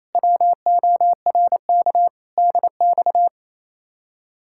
Morse code, heard as a keyed tone, about 34 wpm, 700 Hz. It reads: WORK DX
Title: Speed-Racing Common QSO Phrases Encore - 28wpm